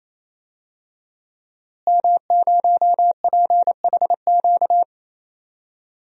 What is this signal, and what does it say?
Morse code, heard as a keyed tone, about 28 wpm, 700 Hz. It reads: M0PHQ
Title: DXCC-100 Call Signs - 5 Characters - Repeated 3x - 28wpm